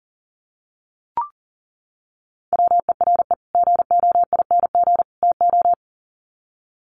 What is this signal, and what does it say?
Morse code, heard as a keyed tone, about 40 wpm, 700 Hz. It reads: WERE GOING TO
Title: Sets of 3 Words 40wpm